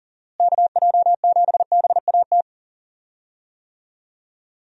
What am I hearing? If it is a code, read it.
Morse code, heard as a keyed tone, about 40 wpm, 700 Hz. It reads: KJ7BAT
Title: USA Call Signs - 6 Characters - Repeated 3x - 40wpm